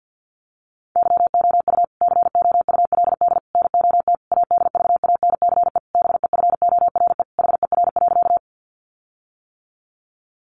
Morse code, heard as a keyed tone, about 50 wpm, 700 Hz. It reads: YOU COULD NOT ADVANCE BEFORE HER.
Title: Sentences from Top 1400 Words 50wpm